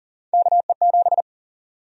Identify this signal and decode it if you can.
Morse code, heard as a keyed tone, about 40 wpm, 700 Hz. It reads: KE7
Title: USA Call Sign Prefixes - 40wpm